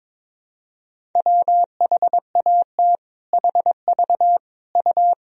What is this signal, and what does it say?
Morse code, heard as a keyed tone, about 21 wpm, 700 Hz. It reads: WHAT HV U
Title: Sentences from Top 400 Words and Abbreviations 22wpm